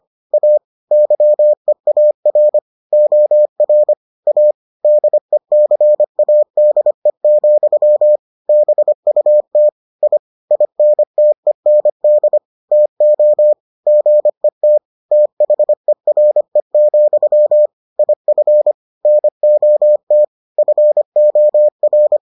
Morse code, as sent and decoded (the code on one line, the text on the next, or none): .- / -.-- . .- .-. / --- .-. / .- / -.. -.-. .- -.. --..-- / -... ..- - / .. / .. -. - . -. -.. / - --- / --. - / - .... .-. --..-- / .. ..-. / -. --- - / ..-. --- .-.
A YEAR OR A DECADE, BUT I INTEND TO GET THERE, IF NOT FOR